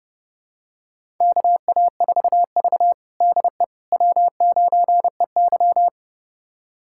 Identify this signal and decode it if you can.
Morse code, heard as a keyed tone, about 30 wpm, 700 Hz. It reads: KA4V DE W9EY
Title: USA Callsign DE Callsign - 4 Characters - 30wpm